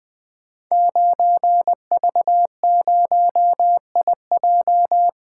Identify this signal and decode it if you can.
Morse code, heard as a keyed tone, about 20 wpm, 700 Hz. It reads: 9V0IJ
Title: Code Groups - 5 Letter:Numbers - 20wpm